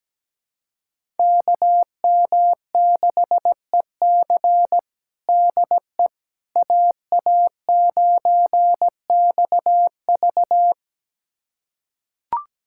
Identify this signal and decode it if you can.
Morse code, heard as a keyed tone, about 17 wpm, 700 Hz. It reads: KM6EC DE AA9XV
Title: USA Callsign DE Callsign - 5 Characters - 17wpm